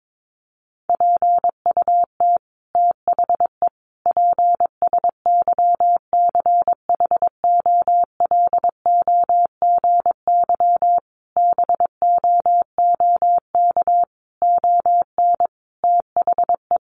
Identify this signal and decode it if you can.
Morse code, heard as a keyed tone, about 22 wpm, 700 Hz. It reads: PUT THE PSYCHOLOGY BOOK ON THE
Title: Sentences from Top 1400 Words 22wpm